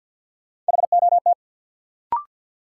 Morse code, heard as a keyed tone, about 52 wpm, 700 Hz. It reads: HOT
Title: Sentences from Top 500 Words - New Word Drill - 50wpm